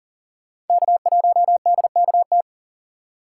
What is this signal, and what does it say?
Morse code, heard as a keyed tone, about 48 wpm, 700 Hz. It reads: K1DKT
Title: Speed Racing - USA Call Signs 40wpm